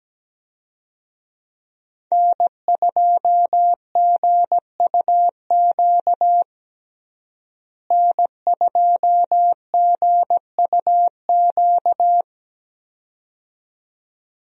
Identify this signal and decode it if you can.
Morse code, heard as a keyed tone, about 17 wpm, 700 Hz. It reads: N2GUQ N2GUQ
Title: USA Call Signs - 5 Characters - Repeated 2x - 17wpm